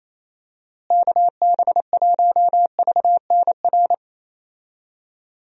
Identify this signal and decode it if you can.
Morse code, heard as a keyed tone, about 28 wpm, 700 Hz. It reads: KB1VNR
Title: USA Call Signs - 6 Characters - Repeated 3x - 28wpm